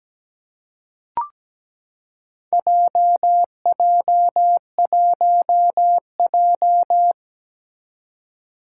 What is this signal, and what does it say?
Morse code, heard as a keyed tone, about 17 wpm, 700 Hz. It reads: JJ1J
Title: Four Challenging Letter-Number Combinations 17wpm